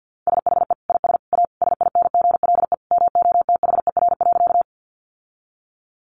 Morse code, heard as a keyed tone, about 48 wpm, 700 Hz. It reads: SHE IS A SINGLE MOTHER.
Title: Sentences from Top 700 Words 50wpm